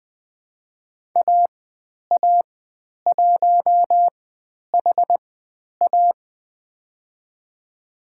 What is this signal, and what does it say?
Morse code, heard as a keyed tone, about 20 wpm, spaced as for 10 wpm, 700 Hz. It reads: AA1HA
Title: 102 Lesson 17g - Letter u - Call Signs 20wpm